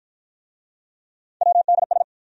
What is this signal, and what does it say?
Morse code, heard as a keyed tone, about 49 wpm, 700 Hz. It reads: WDS